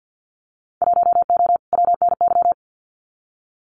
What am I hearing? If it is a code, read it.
Morse code, heard as a keyed tone, about 50 wpm, 700 Hz. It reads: 1O WNY
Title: Winter Field Day Exchanges Encore 50wpm